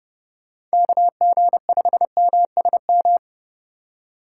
Morse code, heard as a keyed tone, about 32 wpm, 700 Hz. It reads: KG5MSM
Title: USA Call Signs - 6 Characters - 4x Character Spacing - 30wpm